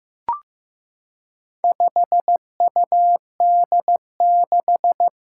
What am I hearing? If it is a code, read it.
Morse code, heard as a keyed tone, about 15 wpm, 700 Hz. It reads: 5UD6
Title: Four Challenging Letter-Number Combinations 15wpm